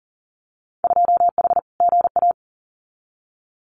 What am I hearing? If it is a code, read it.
Morse code, heard as a keyed tone, about 40 wpm, 700 Hz. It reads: 2H GA
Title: Winter Field Day Exchanges 40wpm